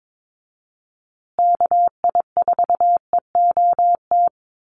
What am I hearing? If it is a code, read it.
Morse code, heard as a keyed tone, about 22 wpm, 700 Hz. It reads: KI4EOT
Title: USA Call Signs Repeated 3x 22wpm